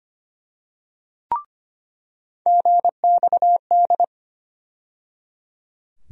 Morse code, heard as a keyed tone, about 25 wpm, 700 Hz. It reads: GXD
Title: USA Call Sign Postfixes - 3 Letters 25wpm